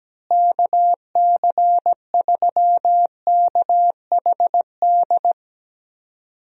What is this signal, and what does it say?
Morse code, heard as a keyed tone, about 17 wpm, 700 Hz. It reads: KC3KHD
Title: USA Call Signs - 6 Characters - Repeated 2x - 17wpm